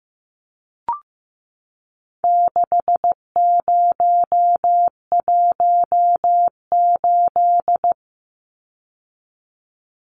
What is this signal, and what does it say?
Morse code, heard as a keyed tone, about 15 wpm, 700 Hz. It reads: 6018
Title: Numbers - 4-Digit 15wpm